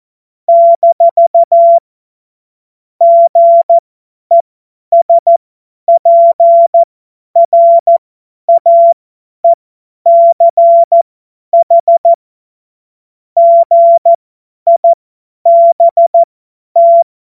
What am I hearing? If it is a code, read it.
Morse code, heard as a keyed tone, about 14 wpm, spaced as for 10 wpm, 680 Hz. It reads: - GESPRAECH GIBT
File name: Morsekurs / Witze / Witz 17. Januar 2023 50 F70